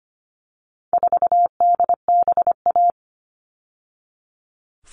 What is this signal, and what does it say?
Morse code, heard as a keyed tone, about 25 wpm, 700 Hz. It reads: 4DBA
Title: All Cut Numbers - 4-Digit 25wpm